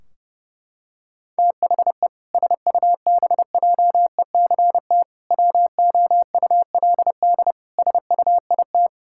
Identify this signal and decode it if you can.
Morse code, heard as a keyed tone, about 30 wpm, 700 Hz. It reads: THE SUBJECT WOULD SUIT
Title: Sentences from Top 1600 Words 30wpm